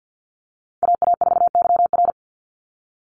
Morse code, heard as a keyed tone, about 50 wpm, 700 Hz. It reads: AA4YR
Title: USA Call Signs 50wpm